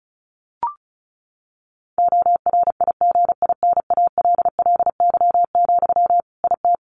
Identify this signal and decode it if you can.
Morse code, heard as a keyed tone, about 34 wpm, 700 Hz. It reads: ORIGINALLY, IT
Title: Sentences from Top 1500 Words 35wpm